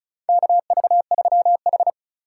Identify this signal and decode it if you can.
Morse code, heard as a keyed tone, about 36 wpm, 700 Hz. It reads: KV3H
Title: USA Call Signs - 3x Character Spacing - 35wpm